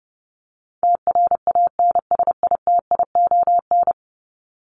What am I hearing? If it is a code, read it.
Morse code, heard as a keyed tone, about 30 wpm, 700 Hz. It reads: TRANSITION
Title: Top 1751-2000 Words 30wpm